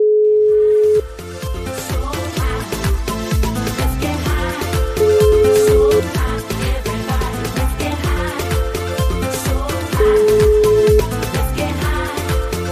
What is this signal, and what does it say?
Ringback tone (Europe, 425 Hz), with noise.